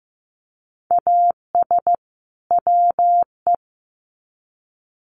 Morse code, heard as a keyed tone, about 15 wpm, 700 Hz. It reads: AS WE
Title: Sets of 2 Words 15wpm